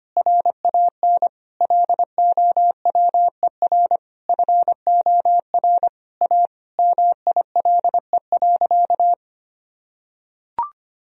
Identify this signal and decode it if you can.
Morse code, heard as a keyed tone, about 25 wpm, 700 Hz. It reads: RAN LOWER FOR A MILE.
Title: Sentences from Top 1000 Words - Review Part 4 - 25wpm